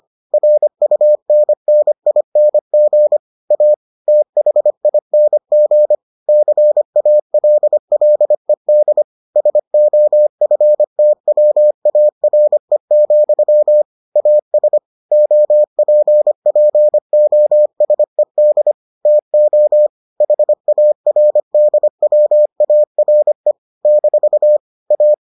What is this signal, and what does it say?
Morse code, heard as a keyed tone, about 25 wpm, 600 Hz. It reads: RUNNING A THING CALLED SOFTWARE, AS OPPOSED TO HARDWARE - A